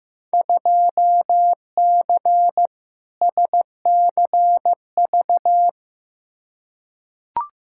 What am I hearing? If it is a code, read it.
Morse code, heard as a keyed tone, about 15 wpm, 700 Hz. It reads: 2C SCV
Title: Field Day Exchanges Encore - 15wpm